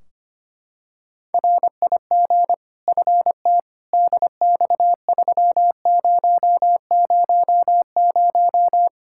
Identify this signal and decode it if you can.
Morse code, heard as a keyed tone, about 25 wpm, 700 Hz. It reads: RIG FT DX3000
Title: Practice QSO - Exchange 2 - Info - 25wpm